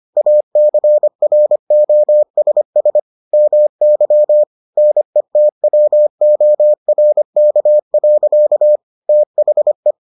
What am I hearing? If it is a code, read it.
Morse code, heard as a keyed tone, about 25 wpm, 600 Hz. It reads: ACROSS MY NETWORK. THE